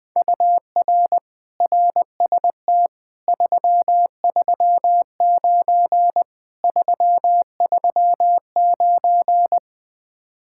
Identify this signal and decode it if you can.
Morse code, heard as a keyed tone, about 20 wpm, 700 Hz. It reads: UR RST 339 339
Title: Speed-Racing Common QSO Phrases - 17wpm